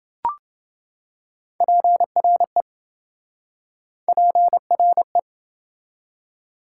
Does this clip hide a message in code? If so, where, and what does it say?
Yes: Morse code from 1.6 to 5.2 s, heard as a keyed tone, about 26 wpm, 700 Hz. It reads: PRE PRE